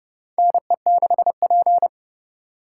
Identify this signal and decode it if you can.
Morse code, heard as a keyed tone, about 30 wpm, 700 Hz. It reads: NE6P
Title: USA Call Signs - 3x Character Spacing - 30wpm